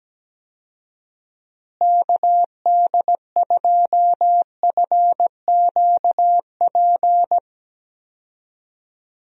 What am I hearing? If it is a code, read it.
Morse code, heard as a keyed tone, about 17 wpm, 700 Hz. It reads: KD2FQP
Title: USA Call Signs - 6 Characters - Repeated 3x - 17wpm